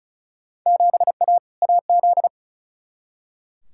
Morse code, heard as a keyed tone, about 34 wpm, 700 Hz. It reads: 7A AZ
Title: Field Day Exchanges - 35wpm